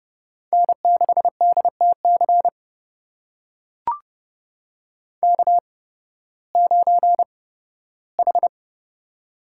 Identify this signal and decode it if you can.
Morse code, heard as a keyed tone, about 32 wpm, spaced as for 16 wpm, 700 Hz. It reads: N6DTC K9H
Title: USA Call Signs - 5 Characters - 8x Character Spacing - 30wpm